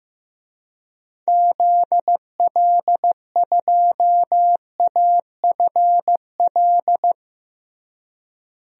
Morse code, heard as a keyed tone, about 15 wpm, 700 Hz. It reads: ZL2AFL
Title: DXCC-100 Call Signs - 6 Characters - Repeated 2x - 15wpm